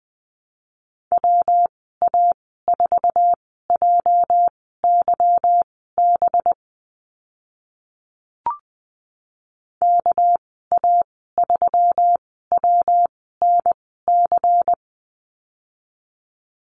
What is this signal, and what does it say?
Morse code, heard as a keyed tone, about 20 wpm, spaced as for 14 wpm, 700 Hz. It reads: WA4JYB KA3WNC
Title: 213 Lesson 34g - Number 8 - Call Signs 20wpm